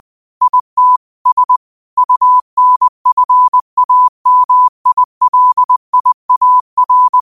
Morse code, heard as a keyed tone, about 20 wpm, 1000 Hz. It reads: IT S UNFAMILIAR